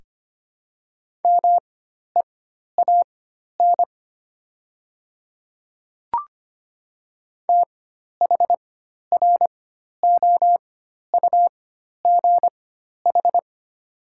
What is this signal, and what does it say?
Morse code, heard as a keyed tone, about 25 wpm, spaced as for 12 wpm, 700 Hz. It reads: MEAN THROUGH
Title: 4x Character Spacing - Top 300 Words - 25wpm